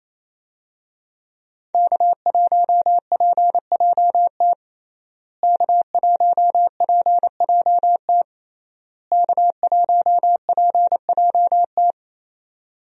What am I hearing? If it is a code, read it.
Morse code, heard as a keyed tone, about 28 wpm, 700 Hz. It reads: K1PJT K1PJT K1PJT
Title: USA Call Signs - 5 Characters - Repeated 3x - 28wpm